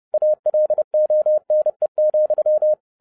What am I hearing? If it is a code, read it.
Morse code, heard as a keyed tone, about 30 wpm, 610 Hz. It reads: ALONE,